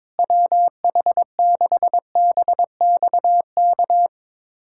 Morse code, heard as a keyed tone, about 22 wpm, 700 Hz. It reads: WH6BXK
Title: USA Callsign DE Callsign - 6 Characters - 22wpm